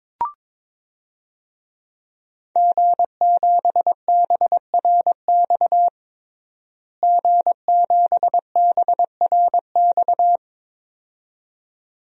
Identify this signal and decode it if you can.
Morse code, heard as a keyed tone, about 22 wpm, 700 Hz. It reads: G7BRX G7BRX
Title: DXCC-100 Call Signs - 5 Characters - Repeated 2x - 22wpm